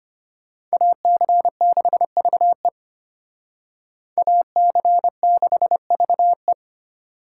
Morse code, heard as a keyed tone, about 32 wpm, 700 Hz. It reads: AC6VE AC6VE